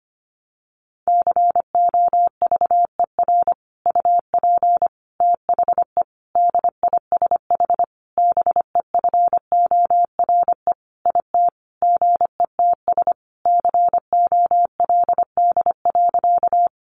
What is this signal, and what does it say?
Morse code, heard as a keyed tone, about 25 wpm, 700 Hz. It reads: COVER UP THE DISH BEFORE IT GETS COLD.
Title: Sentences from Top 1300 Words 25wpm